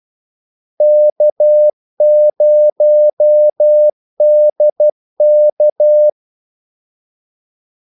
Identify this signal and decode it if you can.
Morse code, heard as a keyed tone, about 12 wpm, 600 Hz. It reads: K0DK
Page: 2 ekstra ordmellemrum mellem kaldesignalerne